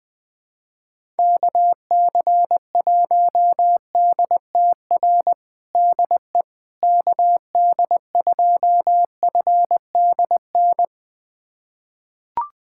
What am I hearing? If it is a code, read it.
Morse code, heard as a keyed tone, about 20 wpm, 700 Hz. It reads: KC1DTR DE KD2FDN
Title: USA Callsign DE Callsign 20wpm